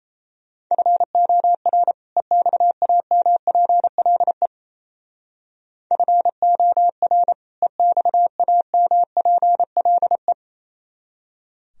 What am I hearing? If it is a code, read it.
Morse code, heard as a keyed tone, about 33 wpm, 700 Hz. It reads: FOR EXAMPLE FOR EXAMPLE
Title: Speed-Racing - Sets of 2 Words 28wpm